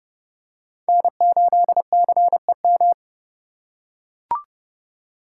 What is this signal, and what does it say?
Morse code, heard as a keyed tone, about 29 wpm, 700 Hz. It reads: N8CEM